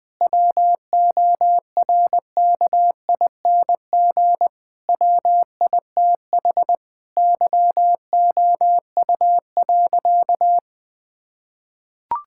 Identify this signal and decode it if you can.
Morse code, heard as a keyed tone, about 21 wpm, 700 Hz. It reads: WORKING WITH YOU.
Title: Sentences with 6-Word Phrases - 20wpm